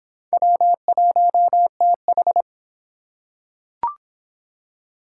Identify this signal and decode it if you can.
Morse code, heard as a keyed tone, about 23 wpm, 700 Hz. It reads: W1TH